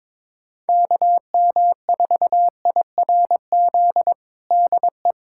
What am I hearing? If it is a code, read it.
Morse code, heard as a keyed tone, about 22 wpm, 700 Hz. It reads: KM4IRZ DE
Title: USA Callsign DE Callsign - 6 Characters - 22wpm